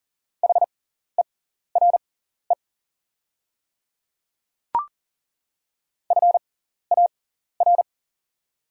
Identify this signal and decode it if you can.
Morse code, heard as a keyed tone, about 40 wpm, spaced as for 14 wpm, 690 Hz. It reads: HERE FAR
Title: 6x Character Spacing - Top 300 Words - 40wpm